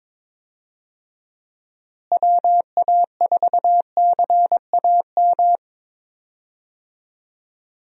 Morse code, heard as a keyed tone, about 22 wpm, 700 Hz. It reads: WA4CAM
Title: USA Call Signs - 6 Characters - Repeated 2x - 22wpm